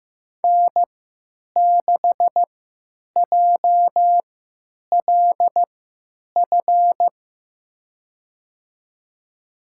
Morse code, heard as a keyed tone, about 16 wpm, spaced as for 9 wpm, 700 Hz. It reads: N6JLF
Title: USA Call Signs - 3x Character Spacing - 15wpm